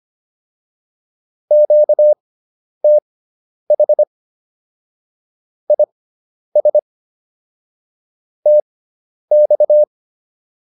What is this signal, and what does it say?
Morse code, heard as a keyed tone, about 25 wpm, spaced as for 10 wpm, 600 Hz. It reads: QTH IS TX